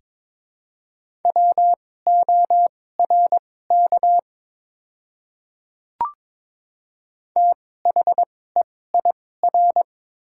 Morse code, heard as a keyed tone, about 22 wpm, spaced as for 16 wpm, 700 Hz. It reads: WORK THEIR
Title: 2x Character Spacing - Top 200 Words - 22wpm